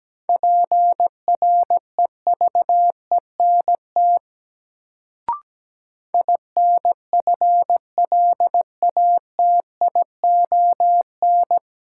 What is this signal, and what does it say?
Morse code, heard as a keyed tone, about 17 wpm, 700 Hz. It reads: PREVENT INFLATION
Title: Sentences from Top 1100 Words - New Word Drill - 17wpm